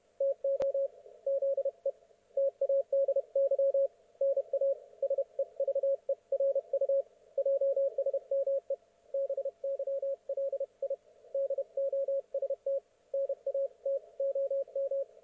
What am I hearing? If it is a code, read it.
Morse code, heard as a keyed tone, about 31 wpm, 530 Hz. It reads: TO ZE TADY NA SEVERU JSME BYLI DOST NA TOM
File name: Archiv / CUC107 / ok1fhi cul107